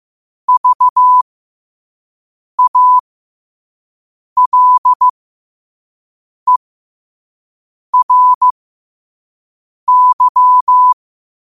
Morse code, heard as a keyed tone, about 15 wpm, spaced as for 5 wpm, 1000 Hz. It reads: VALERY
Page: New quotes every day in morse code at 5 Words per minute.